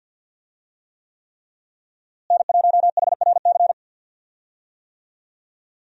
Morse code, heard as a keyed tone, about 52 wpm, 700 Hz. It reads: N1HRC
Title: USA Call Signs - 5 Characters - Repeated 3x - 50wpm